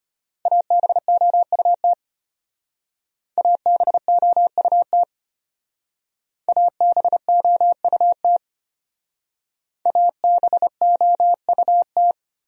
Morse code, heard as a keyed tone, about 31 wpm, 700 Hz. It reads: ABOUT ABOUT ABOUT ABOUT
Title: Speed-Racing Top 300 Words 25wpm